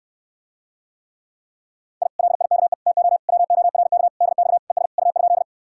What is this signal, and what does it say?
Morse code, heard as a keyed tone, about 69 wpm, 700 Hz. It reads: I LIKE TO WORK DX ES U?